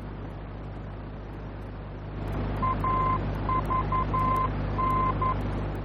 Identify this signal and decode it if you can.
Morse code, heard as a keyed tone, about 11 wpm, 1040 Hz, with noise. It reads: AVN
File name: france / mx / AVN-F-VOR